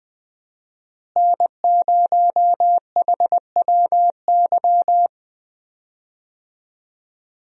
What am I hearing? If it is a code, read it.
Morse code, heard as a keyed tone, about 20 wpm, 700 Hz. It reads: N0HWY
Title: USA Call Signs - 5 Characters - Repeated 3x - 20wpm